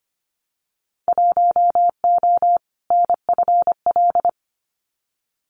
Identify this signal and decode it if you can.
Morse code, heard as a keyed tone, about 25 wpm, 700 Hz. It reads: 1O NFL
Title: Winter Field Day Exchanges 25wpm